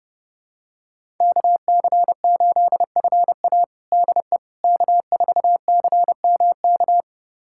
Morse code, heard as a keyed tone, about 29 wpm, 700 Hz. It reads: KC8FA DE K4CMK
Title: USA Callsign DE Callsign - 5 Characters - 30wpm